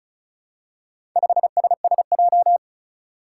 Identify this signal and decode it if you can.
Morse code, heard as a keyed tone, about 36 wpm, 700 Hz. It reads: 5SSJ
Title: Four Challenging Letter-Number Combinations 35wpm